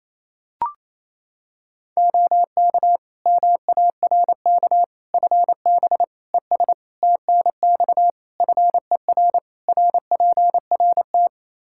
Morse code, heard as a keyed tone, about 27 wpm, 700 Hz. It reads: OK MARK FB ES TNX FER RPRT
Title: Practice QSO - Exchange 2 - Info - 28wpm